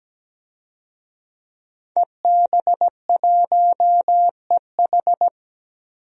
Morse code, heard as a keyed tone, about 17 wpm, 700 Hz. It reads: EB1EH
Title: DXCC-100 Call Signs - 5 Characters - 17wpm